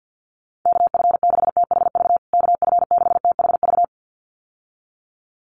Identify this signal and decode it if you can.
Morse code, heard as a keyed tone, about 50 wpm, 700 Hz. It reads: KF6THV KF6THV
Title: USA Call Signs Repeated 2x 50wpm